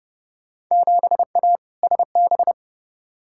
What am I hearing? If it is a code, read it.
Morse code, heard as a keyed tone, about 30 wpm, 700 Hz. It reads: 7A SB
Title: Field Day Exchanges Encore - 30wpm